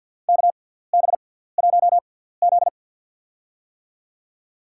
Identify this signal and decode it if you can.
Morse code, heard as a keyed tone, about 52 wpm, spaced as for 17 wpm, 700 Hz. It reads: KB1Z
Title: USA Call Signs - 4 Characters - 6x Character Spacing - 50wpm